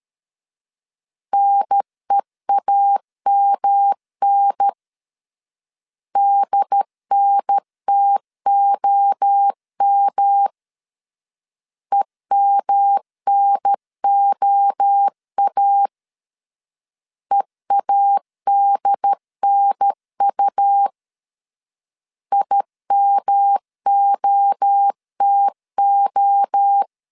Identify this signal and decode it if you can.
Morse code, heard as a keyed tone, about 13 wpm, 800 Hz. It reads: NEAMN DNTOM EMNOA EADNU IMOTO
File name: CW / DET / 1 3 1500C S C 12wpm 2750